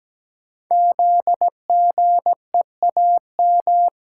Morse code, heard as a keyed tone, about 17 wpm, 700 Hz. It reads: ZGEAM
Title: Code Groups - 5 Letters - 17wpm